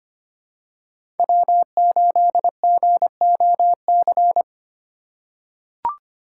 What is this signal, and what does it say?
Morse code, heard as a keyed tone, about 25 wpm, 700 Hz. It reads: W8GOC
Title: USA Call Signs - 5 Characters - 4x Character Spacing - 25wpm